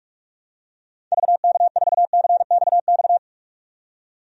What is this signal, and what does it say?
Morse code, heard as a keyed tone, about 46 wpm, 700 Hz. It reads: VK4CXX
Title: DXCC-100 Call Signs - 6 Characters - 45wpm